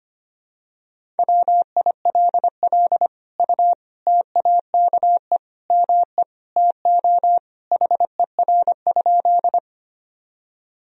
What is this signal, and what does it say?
Morse code, heard as a keyed tone, about 25 wpm, 700 Hz. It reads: WILL U TAKE ME TO HER?
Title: Sentences from Top 200 Words and Abbreviations 25wpm